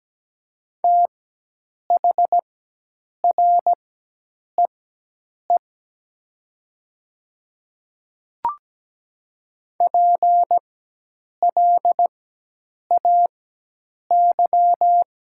Morse code, heard as a keyed tone, about 17 wpm, spaced as for 8 wpm, 700 Hz. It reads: THREE PLAY
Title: 4x Character Spacing - Top 200 Words - 17wpm